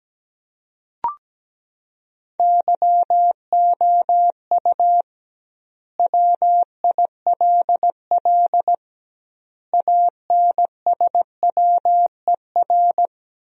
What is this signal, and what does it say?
Morse code, heard as a keyed tone, about 17 wpm, 700 Hz. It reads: YOU WILL ANSWER
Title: 2x Word-Spacing - Sentences from Top 300 Words - 17wpm